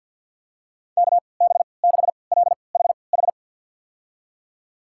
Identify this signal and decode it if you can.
Morse code, heard as a keyed tone, about 49 wpm, spaced as for 28 wpm, 700 Hz. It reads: KB6LHH
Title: USA Call Signs - 6 Characters - 3x Character Spacing - 50wpm